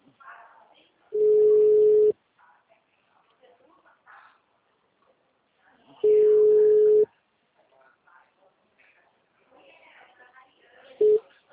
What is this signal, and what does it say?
Ringback tone (Europe, 425 Hz), with speech.
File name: out- -2042- - - .20